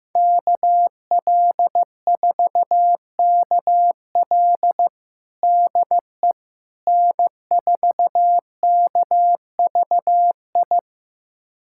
Morse code, heard as a keyed tone, about 16 wpm, 700 Hz. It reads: KL4KL DE N4KVI
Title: USA Callsign DE Callsign - 5 Characters - 15wpm